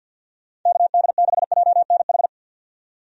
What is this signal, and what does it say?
Morse code, heard as a keyed tone, about 50 wpm, 700 Hz. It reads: KD6JNH
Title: USA Call Signs - 6 Characters - 50wpm